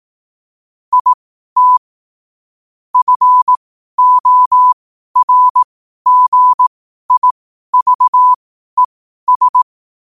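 Morse code, heard as a keyed tone, about 18 wpm, spaced as for 12 wpm, 1000 Hz. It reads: IT FORGIVES
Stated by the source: Morse code quotes at 12 words per minute, by ai4qr.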